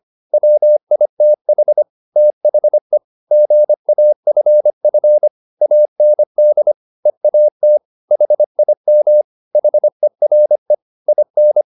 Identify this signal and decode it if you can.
Morse code, heard as a keyed tone, about 25 wpm, 600 Hz. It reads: WITH THE GAFF AND EAT HIM HERE IN
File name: the old man and the sea part 1 25wpm